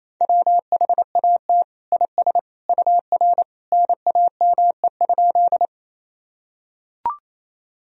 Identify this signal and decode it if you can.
Morse code, heard as a keyed tone, about 27 wpm, 700 Hz. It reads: WHAT IS UR NAME?
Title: Sentences from Top 200 Words and Abbreviations 28wpm